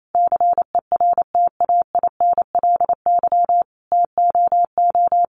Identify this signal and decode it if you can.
Morse code, heard as a keyed tone, about 28 wpm, 690 Hz. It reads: CERTAINLY TOO
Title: Sentences from Top 900 Words 28wpm